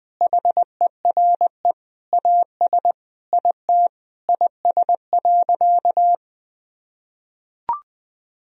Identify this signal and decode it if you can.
Morse code, heard as a keyed tone, about 20 wpm, 700 Hz. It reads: HERE AS IT IS.
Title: Sentences from Top 1000 Words - Review Part 3 - 20wpm